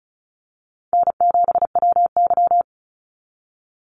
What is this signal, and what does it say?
Morse code, heard as a keyed tone, about 34 wpm, 700 Hz. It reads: N7WY
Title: Call Signs Repeated 2x - 35wpm